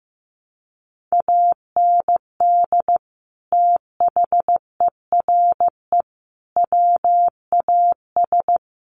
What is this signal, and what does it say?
Morse code, heard as a keyed tone, about 15 wpm, 700 Hz. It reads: AND THERE WAS
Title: Sets of 3 Words 15wpm